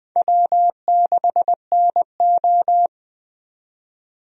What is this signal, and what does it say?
Morse code, heard as a keyed tone, about 21 wpm, 700 Hz. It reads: W6NO
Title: USA Call Signs - 4 Characters - 20wpm